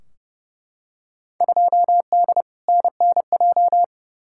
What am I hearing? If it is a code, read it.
Morse code, heard as a keyed tone, about 30 wpm, 700 Hz. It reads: 2D NNJ
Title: Field Day Exchanges - 30wpm